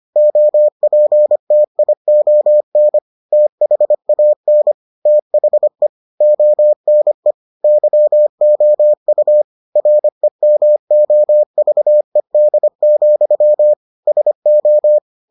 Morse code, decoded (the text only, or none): OPTION THAN THE ONE YOU REMOVED, SO